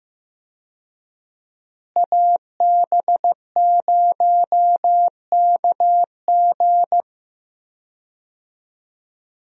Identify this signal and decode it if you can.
Morse code, heard as a keyed tone, about 15 wpm, 700 Hz. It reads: AB0KG
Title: USA Call Signs - 5 Characters - Repeated 3x - 15wpm